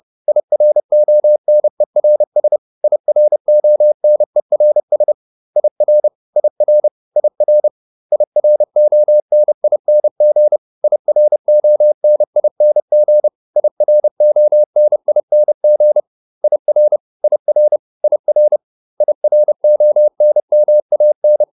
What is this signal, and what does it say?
Morse code, heard as a keyed tone, about 30 wpm, 600 Hz. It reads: IRONERS IRONERS IR IR IR IRONING IRONING IRONING IR IR IR IRONMAN